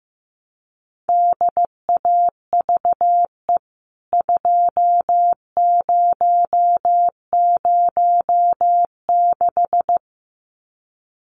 Morse code, heard as a keyed tone, about 15 wpm, 700 Hz. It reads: DAVE 2006
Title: CWT Exchange 15wpm